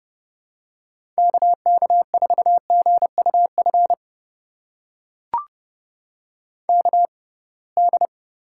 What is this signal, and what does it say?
Morse code, heard as a keyed tone, about 30 wpm, spaced as for 23 wpm, 700 Hz. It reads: KK4GUF KD
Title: USA Call Signs - 6 Characters - 6x Character Spacing - 30wpm